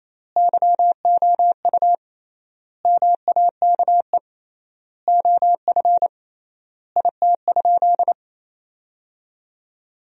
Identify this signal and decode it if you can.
Morse code, heard as a keyed tone, about 28 wpm, 700 Hz. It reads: YOU MAKE OF IT?
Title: 3x Word-Spacing - Sentences from Top 100 Words - 28wpm